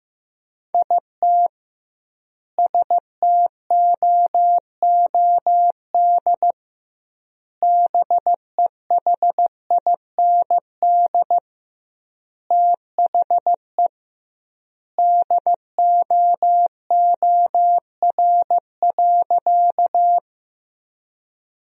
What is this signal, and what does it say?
Morse code, heard as a keyed tone, about 16 wpm, 700 Hz. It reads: IT STOOD BEHIND THE DOOR.
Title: 2x Word-Spacing - Sentences from Top 500 Words - 15wpm